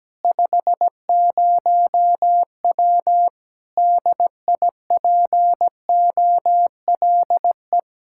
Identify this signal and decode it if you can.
Morse code, heard as a keyed tone, about 17 wpm, 700 Hz. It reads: 50W DIPOLE
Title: Practice QSO - Exchange 2 - Info - 17wpm